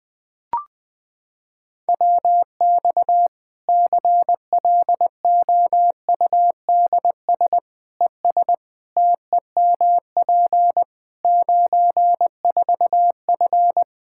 Morse code, heard as a keyed tone, about 21 wpm, 700 Hz. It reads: WX CLOUDS ES TEMP 94F
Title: Practice QSO - Exchange 2 - Info - 20wpm